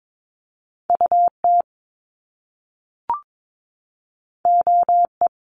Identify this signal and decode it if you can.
Morse code, heard as a keyed tone, about 22 wpm, 700 Hz. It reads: UT OE
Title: Top Two Letter Combinations in Words 22wpm